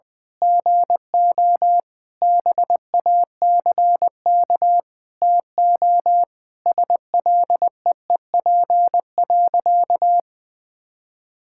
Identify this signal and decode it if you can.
Morse code, heard as a keyed tone, about 19 wpm, 700 Hz. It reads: GO BACK TO SLEEP.
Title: Sentences with 7-Word Phrases - 20wpm